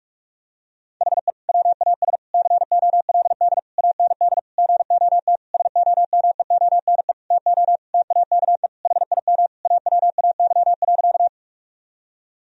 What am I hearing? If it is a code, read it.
Morse code, heard as a keyed tone, about 46 wpm, 700 Hz. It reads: HE WAS COLD AND GOT SOMEONE TO TAKE HIM AWAY.
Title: Sentences from Top 1000 Words - Review Part 4 - 45wpm